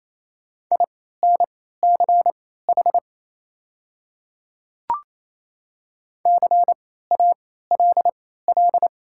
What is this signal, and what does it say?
Morse code, heard as a keyed tone, about 28 wpm, spaced as for 16 wpm, 690 Hz. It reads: INCH CALL
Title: 3x Character Spacing - Top 500 Words - 28wpm